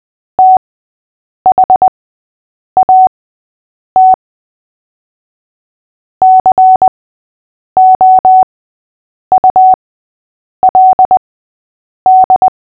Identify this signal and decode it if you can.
Morse code, heard as a keyed tone, about 20 wpm, spaced as for 8 wpm, 750 Hz. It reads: THAT COULD
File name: cw / fists / Short-Story-6-8wpm